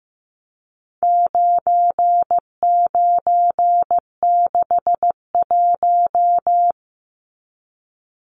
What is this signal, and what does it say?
Morse code, heard as a keyed tone, about 15 wpm, 700 Hz. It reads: 9961
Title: Numbers - 4-Digit 15wpm